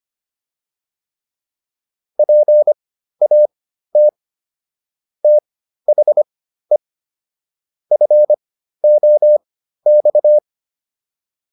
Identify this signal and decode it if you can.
Morse code, heard as a keyed tone, about 25 wpm, spaced as for 13 wpm, 600 Hz. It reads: PAT THE FOX